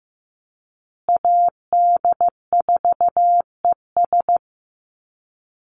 Morse code, heard as a keyed tone, about 15 wpm, 700 Hz. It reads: AD4ES
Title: Call Signs Repeated 3x - 15wpm